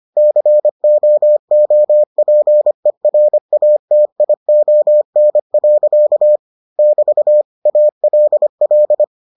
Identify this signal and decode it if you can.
Morse code, heard as a keyed tone, about 25 wpm, 600 Hz. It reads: COOPERATION. <BT> ALL